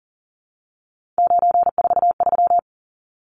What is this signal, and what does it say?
Morse code, heard as a keyed tone, about 40 wpm, 700 Hz. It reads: 943
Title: Numbers - 3-Digit 40wpm